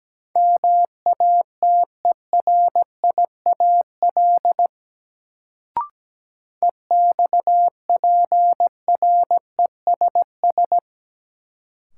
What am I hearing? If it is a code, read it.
Morse code, heard as a keyed tone, about 17 wpm, 700 Hz. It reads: MATERIAL EXPRESS
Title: Sentences from Top 1100 Words - New Word Drill Alt - 17wpm